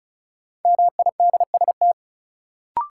Morse code, heard as a keyed tone, about 36 wpm, 700 Hz. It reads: MIDST
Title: Sets of 4 Words - New Word Drill - 35wpm